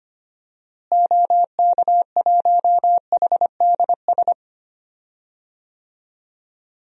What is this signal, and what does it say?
Morse code, heard as a keyed tone, about 25 wpm, 700 Hz. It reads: OK1HDS
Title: DXCC-100 Call Signs - 6 Characters - Repeated 3x - 25wpm